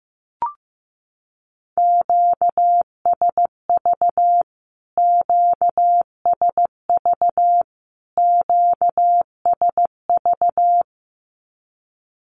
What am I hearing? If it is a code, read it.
Morse code, heard as a keyed tone, about 15 wpm, 700 Hz. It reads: QSV QSV QSV
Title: Q-Codes Repeated 3x 15wpm